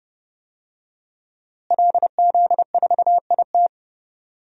Morse code, heard as a keyed tone, about 30 wpm, 700 Hz. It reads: LZ4IT
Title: DXCC-100 Call Signs - 5 Characters - Repeated 2x - 30wpm